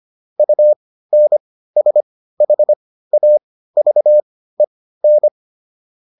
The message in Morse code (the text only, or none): UNSHAVEN